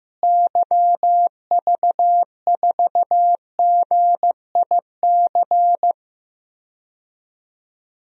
Morse code, heard as a keyed tone, about 16 wpm, 700 Hz. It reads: YV4GIC
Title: DXCC-100 Call Signs - 6 Characters - Repeated 3x - 15wpm